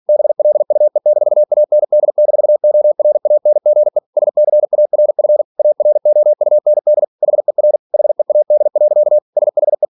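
Morse code, heard as a keyed tone, about 46 wpm, 600 Hz. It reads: BLUE-AND-ORANGE SCARF AROUND HER HEAD. SHE
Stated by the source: Practice text from an English novel sent at 47 WPM.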